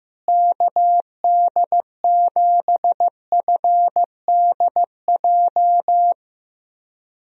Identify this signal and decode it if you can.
Morse code, heard as a keyed tone, about 15 wpm, 700 Hz. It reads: KD7FDJ
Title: USA Call Signs - 6 Characters - Repeated 3x - 15wpm